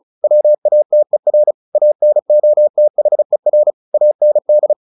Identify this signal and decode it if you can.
Morse code, heard as a keyed tone, about 34 wpm, 600 Hz. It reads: WATER ANOTHER AND
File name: the old man and the sea / the old man and the sea 1 35wpm